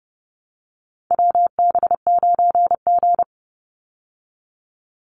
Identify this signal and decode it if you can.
Morse code, heard as a keyed tone, about 30 wpm, 700 Hz. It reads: WB9G
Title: Call Signs - 30wpm